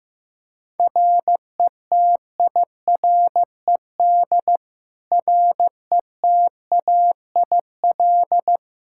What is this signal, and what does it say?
Morse code, heard as a keyed tone, about 15 wpm, 700 Hz. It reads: RETIRED RETAIL
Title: Practice QSO - Exchange 3 - More Info - 15wpm